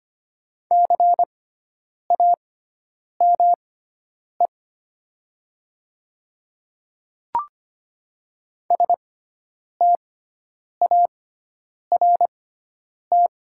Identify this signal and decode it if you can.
Morse code, heard as a keyed tone, about 25 wpm, spaced as for 9 wpm, 700 Hz. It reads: CAME START
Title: 6x Character Spacing - Top 300 Words - 25wpm